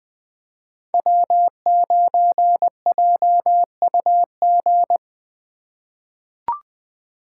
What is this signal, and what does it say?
Morse code, heard as a keyed tone, about 20 wpm, 700 Hz. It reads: W9JUG
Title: K1USN SST QSO Practice - 20wpm